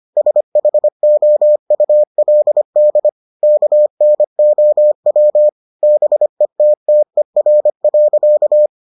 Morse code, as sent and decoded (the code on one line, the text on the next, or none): ... .... --- ..- .-.. -.. / -.- -. --- .-- / -... - - . .-. .-.-.-
SHOULD KNOW BETTER.